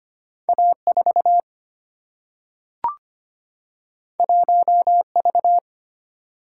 Morse code, heard as a keyed tone, about 25 wpm, 700 Hz. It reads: A4 1V
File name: Two Letter-Number Combinations - 25wpm